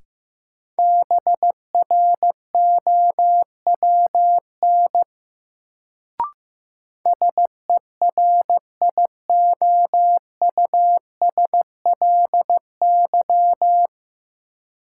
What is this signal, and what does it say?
Morse code, heard as a keyed tone, about 15 wpm, 700 Hz. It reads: BROWN SERIOUSLY
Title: Sentences from Top 1200 Words - New Word Drill - 15wpm